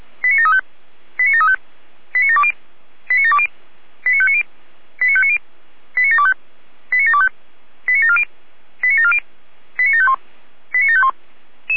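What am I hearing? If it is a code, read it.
Five-tone selective call (ZVEI), heard as quick runs of five tones, tones of about 70 ms. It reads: 89735 89735 89720 89720 89590 89590 89725 89725 89740 89740 89751 89751
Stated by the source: ZVEI five-tone selective call 89706, received by SDR on the 455 channel.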